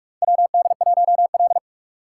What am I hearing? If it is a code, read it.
Morse code, heard as a keyed tone, about 46 wpm, 700 Hz. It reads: WD1L
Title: USA Call Signs - 4 Characters - 4x Character Spacing - 45wpm